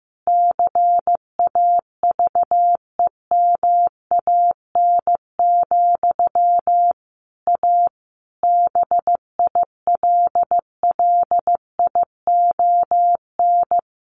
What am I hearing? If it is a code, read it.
Morse code, heard as a keyed tone, about 15 wpm, 700 Hz. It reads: CAVEMAN, A BILLION
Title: A Sound Of Thunder 15wpm